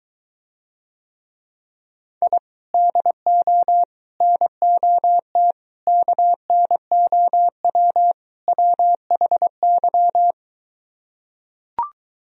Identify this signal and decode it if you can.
Morse code, heard as a keyed tone, about 23 wpm, 700 Hz. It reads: I DO NOT KNOW WHY